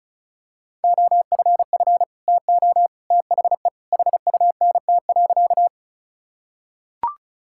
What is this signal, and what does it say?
Morse code, heard as a keyed tone, about 36 wpm, 700 Hz. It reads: OFF TO THE HUNT.
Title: Sentences from Top 1600 Words 35wpm